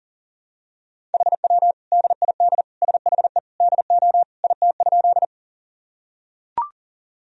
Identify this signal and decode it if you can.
Morse code, heard as a keyed tone, about 40 wpm, 700 Hz. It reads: HW DID SHE DO IT?
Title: Sentences from Top 100 Words and Abbreviations 40wpm